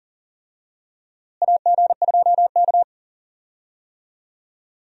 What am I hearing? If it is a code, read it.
Morse code, heard as a keyed tone, about 37 wpm, 700 Hz. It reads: AG2K
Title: USA Call Signs - 4 Characters - Repeated 3x - 40wpm